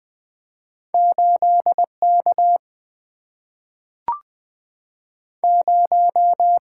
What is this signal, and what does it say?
Morse code, heard as a keyed tone, about 20 wpm, 700 Hz. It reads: 8K 0
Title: Single-Letter Single-Number 20wpm